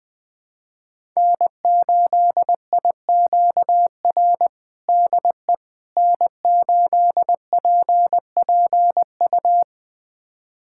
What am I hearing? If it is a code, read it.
Morse code, heard as a keyed tone, about 21 wpm, 700 Hz. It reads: N8IQR DE N8PPU
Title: USA Callsign DE Callsign - 5 Characters - 20wpm